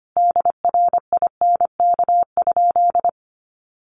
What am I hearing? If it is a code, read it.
Morse code, heard as a keyed tone, about 25 wpm, 700 Hz. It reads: DRINK?
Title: Sentences from Top 1500 Words 25wpm